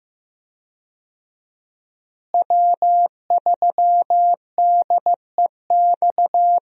Morse code, heard as a keyed tone, about 15 wpm, 700 Hz. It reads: W3DEX
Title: USA Call Signs - 5 Characters - Repeated 2x - 15wpm